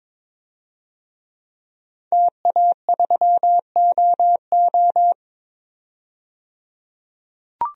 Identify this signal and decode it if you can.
Morse code, heard as a keyed tone, about 22 wpm, 700 Hz. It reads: TA3OO
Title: DXCC-100 Call Signs - 5 Characters - 22wpm